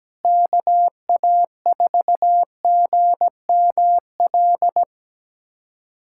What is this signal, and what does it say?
Morse code, heard as a keyed tone, about 17 wpm, 700 Hz. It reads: KA4GML
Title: USA Callsign DE Callsign - 6 Characters - 17wpm